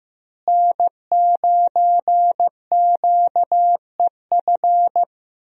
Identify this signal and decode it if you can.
Morse code, heard as a keyed tone, about 15 wpm, 700 Hz. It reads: N9QEF
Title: USA Call Signs - 5 Characters - Repeated 2x - 15wpm